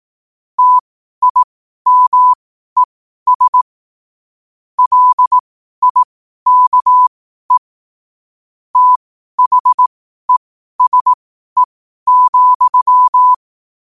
Morse code, heard as a keyed tone, about 18 wpm, spaced as for 12 wpm, 1000 Hz. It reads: TIMES LIKE THESE,
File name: qotdfiles / 12wpm